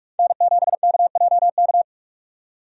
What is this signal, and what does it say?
Morse code, heard as a keyed tone, about 46 wpm, 700 Hz. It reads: N7KJK
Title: USA Call Signs - 5 Characters - 45wpm